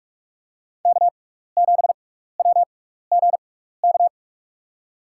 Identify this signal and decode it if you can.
Morse code, heard as a keyed tone, about 46 wpm, spaced as for 15 wpm, 700 Hz. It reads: K7WGK
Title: USA Call Signs - 5 Characters - 6x Character Spacing - 45wpm